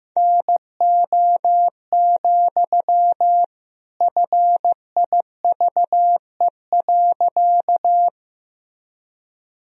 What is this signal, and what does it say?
Morse code, heard as a keyed tone, about 16 wpm, 700 Hz. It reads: NO, FIVE.
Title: Sentences from Top 1000 Words - Review Part 2 - 15wpm